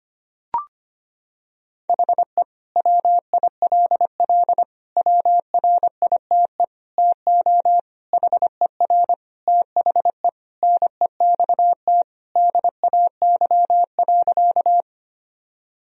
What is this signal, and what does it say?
Morse code, heard as a keyed tone, about 25 wpm, 700 Hz. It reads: HE WILL WRITE TO HER THE NEXT DAY.
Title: Sentences from Top 1000 Words - Review Part 1 - 25wpm